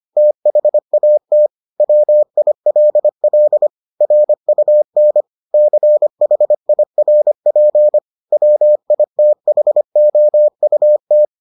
Morse code, heard as a keyed tone, about 25 wpm, 600 Hz. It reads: THAT WILL RUN CHIRP WITHOUT